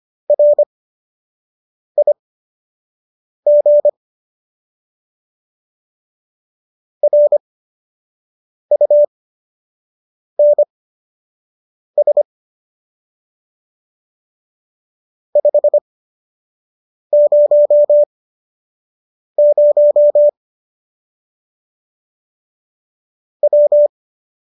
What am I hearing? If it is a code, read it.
Morse code, heard as a keyed tone, about 25 wpm, spaced as for 6 wpm, 590 Hz. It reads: RIG RUNS 500 W